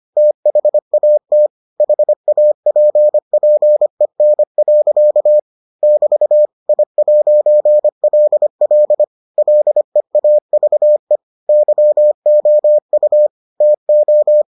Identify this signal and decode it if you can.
Morse code, heard as a keyed tone, about 25 wpm, 600 Hz. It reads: THAT HAPPEN. <BT> I'LL LEAVE YOU TO